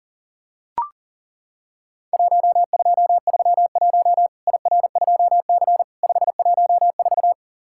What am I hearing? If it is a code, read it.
Morse code, heard as a keyed tone, about 40 wpm, 700 Hz. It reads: 1231 IR2C 514
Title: Worked All Europe DX Contest - QTC Exchange - 40wpm